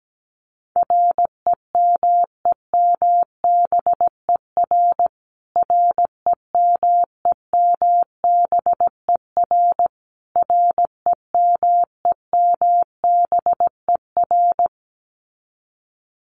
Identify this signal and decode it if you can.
Morse code, heard as a keyed tone, about 17 wpm, 700 Hz. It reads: REMEMBER REMEMBER REMEMBER
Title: Top 500 Words Repeated 17wpm